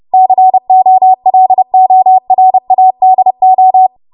Morse code, heard as a keyed tone, about 30 wpm, 750 Hz. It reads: COLORADO
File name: Morse / 30WPM